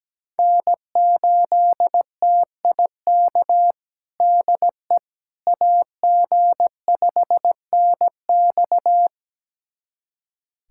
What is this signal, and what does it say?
Morse code, heard as a keyed tone, about 17 wpm, 700 Hz. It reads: N8TIK DE AG5NX
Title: USA Callsign DE Callsign - 5 Characters - 17wpm